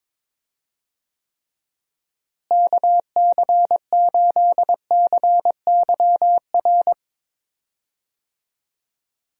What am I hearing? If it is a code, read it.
Morse code, heard as a keyed tone, about 22 wpm, 700 Hz. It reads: KC8CYR
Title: USA Call Signs - 6 Characters - Repeated 3x - 22wpm